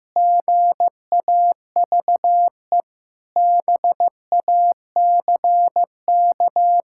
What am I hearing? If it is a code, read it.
Morse code, heard as a keyed tone, about 16 wpm, 700 Hz. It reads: GAVE BACK
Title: Pangram Challenge v2 - 15wpm